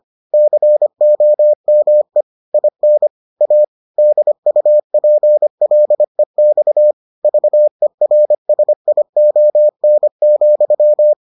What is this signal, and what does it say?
Morse code, heard as a keyed tone, about 25 wpm, 600 Hz. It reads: COME IN A DUPLEX VERSION,